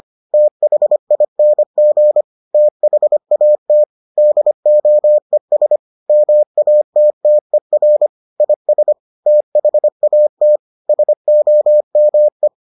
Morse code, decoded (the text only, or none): THING THAT DOES MATTER IS THAT SOME